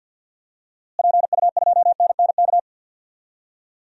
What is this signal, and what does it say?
Morse code, heard as a keyed tone, about 53 wpm, 700 Hz. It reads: PU2NNK
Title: DXCC-100 Call Signs - 6 Characters - Repeated 2x - 50wpm